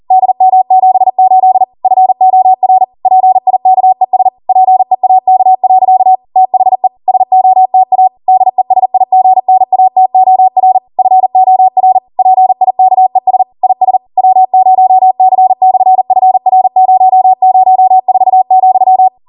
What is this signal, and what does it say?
Morse code, heard as a keyed tone, about 40 wpm, 750 Hz. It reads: DM78 FOR PIKES PEAK. THE SOTA DESIGNATOR FOR PIKES IS W0C/FR004,